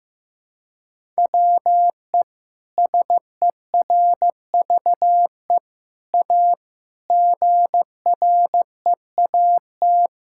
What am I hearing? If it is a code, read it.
Morse code, heard as a keyed tone, about 15 wpm, 700 Hz. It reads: WE SERVE A GREAT
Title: Sentences from Top 1000 Words - Review Part 6 - 15wpm